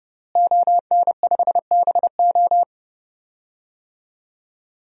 Morse code, heard as a keyed tone, about 30 wpm, 700 Hz. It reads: ON5BO
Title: DXCC-100 Call Signs - 5 Characters - Repeated 2x - 30wpm